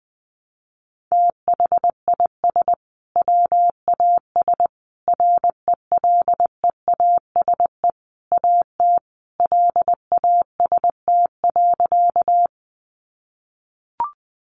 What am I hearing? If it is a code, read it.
Morse code, heard as a keyed tone, about 20 wpm, 700 Hz. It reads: THIS WAS RELEASE AT LAST.
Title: Sentences from Top 1100 Words 20wpm